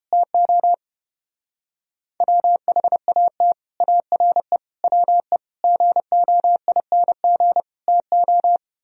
Morse code, heard as a keyed tone, about 30 wpm, 700 Hz. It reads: TO WHAT ARE WE GOING TO